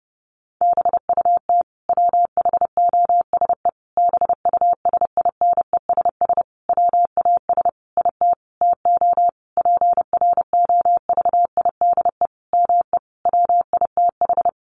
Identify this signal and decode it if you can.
Morse code, heard as a keyed tone, about 30 wpm, 700 Hz. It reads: BUT WHOSE BUSINESS WAS IT TO PROVIDE ME WITH
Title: Sentences from Top 1500 Words 30wpm